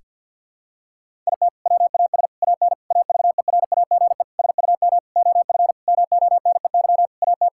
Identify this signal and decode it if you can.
Morse code, heard as a keyed tone, about 47 wpm, 700 Hz. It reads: IT WAS AN AVERAGE SUM OF MONEY AT
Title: Sentences from Top 1000 Words - Review Part 6 - 50wpm